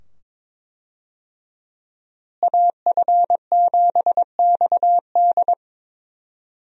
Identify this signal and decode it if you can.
Morse code, heard as a keyed tone, about 22 wpm, 700 Hz. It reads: AF7XD
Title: USA Call Signs - 5 Characters - Repeated 2x - 22wpm